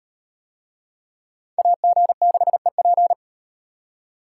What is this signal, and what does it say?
Morse code, heard as a keyed tone, about 37 wpm, 700 Hz. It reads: AG6EP